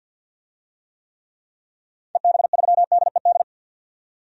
Morse code, heard as a keyed tone, about 51 wpm, 700 Hz. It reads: EB3DED
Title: DXCC-100 Call Signs - 6 Characters - 50wpm